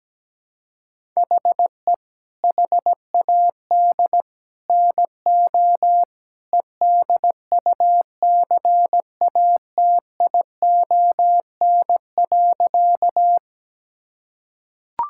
Morse code, heard as a keyed tone, about 17 wpm, 700 Hz. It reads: HE HAD NO EDUCATION.
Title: Sentences from Top 1000 Words - Review Part 6 - 17wpm